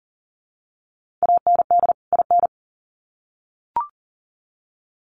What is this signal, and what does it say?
Morse code, heard as a keyed tone, about 40 wpm, 700 Hz. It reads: AND IN
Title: Sets of 2 Words 40wpm